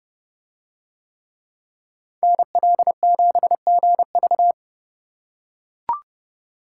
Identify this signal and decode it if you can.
Morse code, heard as a keyed tone, about 29 wpm, 700 Hz. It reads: NL7GV